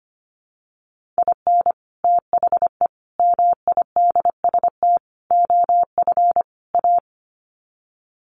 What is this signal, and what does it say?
Morse code, heard as a keyed tone, about 25 wpm, 700 Hz. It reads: IN THE MIDST OF A
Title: Sets of 5 Words 25wpm